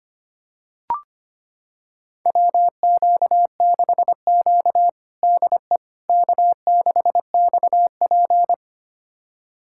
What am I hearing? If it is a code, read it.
Morse code, heard as a keyed tone, about 25 wpm, 700 Hz. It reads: WQ6Q DE K6XP
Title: USA Callsign DE Callsign - 4 Characters - 25wpm